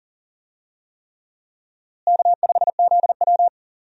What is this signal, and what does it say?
Morse code, heard as a keyed tone, about 40 wpm, 700 Hz. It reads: K5ZW
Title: USA Call Signs - 4 Characters - Repeated 3x - 40wpm